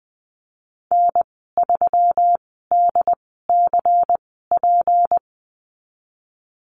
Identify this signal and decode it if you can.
Morse code, heard as a keyed tone, about 19 wpm, spaced as for 14 wpm, 700 Hz. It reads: N3DCP
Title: 245 Lesson 39g - Letter z - Call Signs 20wpm